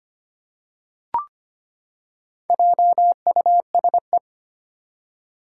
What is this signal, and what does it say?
Morse code, heard as a keyed tone, about 25 wpm, 700 Hz. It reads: JUSE
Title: Four Challenging Letter-Number Combinations 25wpm